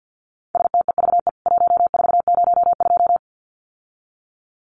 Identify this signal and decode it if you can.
Morse code, heard as a keyed tone, about 50 wpm, 700 Hz. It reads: STEVE 1402
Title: CWT Exchange 50wpm